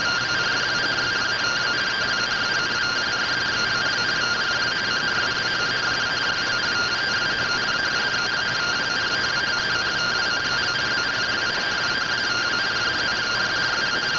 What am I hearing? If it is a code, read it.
Two-tone teleprinter signal, about 170 Hz shift.